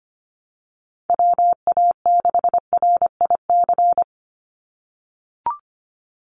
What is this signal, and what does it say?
Morse code, heard as a keyed tone, about 25 wpm, 700 Hz. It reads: WA6RIC
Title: NA SOTA Chaser Callsigns 25wpm